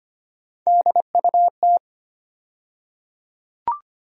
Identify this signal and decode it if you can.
Morse code, heard as a keyed tone, about 25 wpm, 700 Hz. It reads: DUT
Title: USA Call Sign Postfixes - 3 Letters 25wpm